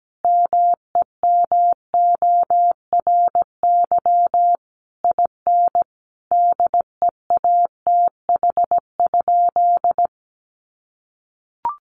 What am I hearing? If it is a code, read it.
Morse code, heard as a keyed tone, about 17 wpm, 700 Hz. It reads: MEMORY IN DEATH?
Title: Sentences from Top 1300 Words 17wpm